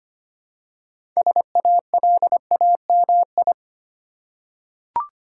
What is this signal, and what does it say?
Morse code, heard as a keyed tone, about 25 wpm, 690 Hz. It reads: SALAMI
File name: practice / 25wpm / Easy Words with Hints - 25wpm